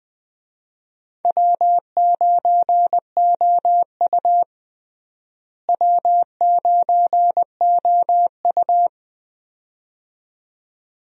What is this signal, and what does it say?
Morse code, heard as a keyed tone, about 20 wpm, 700 Hz. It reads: W9OU W9OU
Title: USA Call Signs - 4 Characters - Repeated 2x - 20wpm